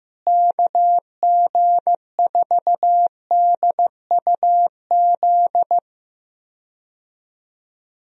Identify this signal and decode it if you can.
Morse code, heard as a keyed tone, about 15 wpm, 700 Hz. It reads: KG4DUZ
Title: USA Call Signs - 6 Characters - Repeated 3x - 15wpm